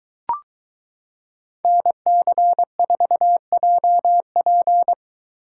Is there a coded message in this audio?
Yes: Morse code from 1.6 to 4.9 s, heard as a keyed tone, about 23 wpm, 700 Hz. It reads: NC4JP